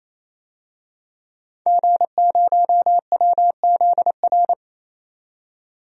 Morse code, heard as a keyed tone, about 28 wpm, 700 Hz. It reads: G0WZR
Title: DXCC-100 Call Signs - 5 Characters - 28wpm